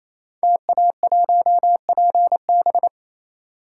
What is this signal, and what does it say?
Morse code, heard as a keyed tone, about 27 wpm, 700 Hz. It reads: TA1PB
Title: Code Groups - 5 Letter:Numbers - 28wpm